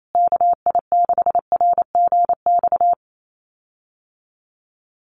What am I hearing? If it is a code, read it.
Morse code, heard as a keyed tone, about 28 wpm, 700 Hz. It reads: KI6RGX
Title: USA Call Signs Repeated 2x 28wpm